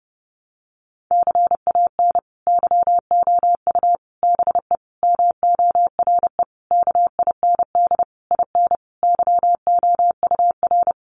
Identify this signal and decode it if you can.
Morse code, heard as a keyed tone, about 28 wpm, 700 Hz. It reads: CAN YOU BE MORE KIND IN YOUR
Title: Sentences from Top 1100 Words 30wpm